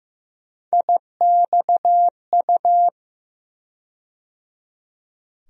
Morse code, heard as a keyed tone, about 16 wpm, 700 Hz. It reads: IXU
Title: USA Call Sign Postfixes - 3 Letters 15wpm